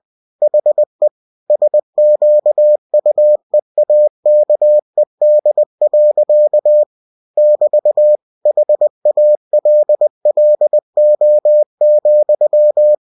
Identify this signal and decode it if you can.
Morse code, heard as a keyed tone, about 20 wpm, 600 Hz. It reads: HE SQUEAKED. <BT> HALLO,
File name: cw-books / winnie-the-pooh / winnie-20wpm-page-0106